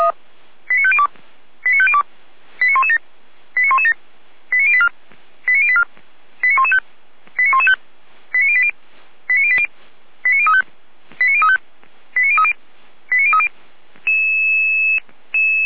Five-tone selective call (ZVEI), heard as quick runs of five tones, tones of about 70 ms. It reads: 89502 89502 89117 89117 89084 89084 89115 89115 89080 89080 89035 89035 89030 89030
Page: /455/SDR/ZVEI/